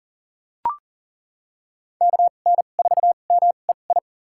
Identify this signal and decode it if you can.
Morse code, heard as a keyed tone, about 40 wpm, spaced as for 29 wpm, 700 Hz. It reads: KN4MEI
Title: USA Call Signs - 2x Character Spacing - 40wpm